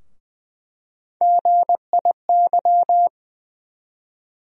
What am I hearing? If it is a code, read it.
Morse code, heard as a keyed tone, about 20 wpm, 700 Hz. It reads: GIY
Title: USA Call Sign Postfixes - 3 Letters 20wpm